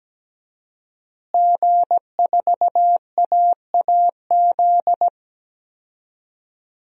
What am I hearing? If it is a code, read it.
Morse code, heard as a keyed tone, about 17 wpm, 700 Hz. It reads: G4AAZ
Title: DXCC-100 Call Signs - 5 Characters - Repeated 2x - 17wpm